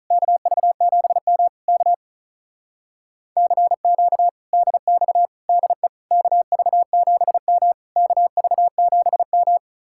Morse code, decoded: KV7M K CQ DX DE KV7M KV7M

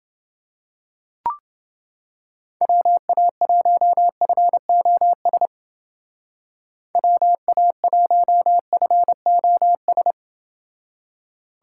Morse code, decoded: WA1FOS WA1FOS

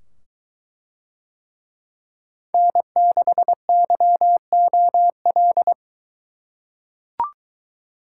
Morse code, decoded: N6YOL

